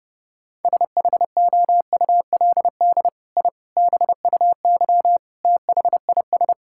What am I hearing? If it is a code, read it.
Morse code, heard as a keyed tone, about 30 wpm, 700 Hz. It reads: SHOULD I BUY THIS
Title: Sentences from Top 1000 Words - Review Part 5 - 30wpm